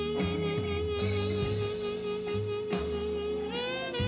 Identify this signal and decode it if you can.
Congestion tone (425 Hz), with noise.